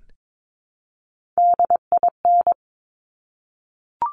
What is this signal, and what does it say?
Morse code, heard as a keyed tone, about 22 wpm, 700 Hz. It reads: DIN
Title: Top Three Letter Combinations in Words 22wpm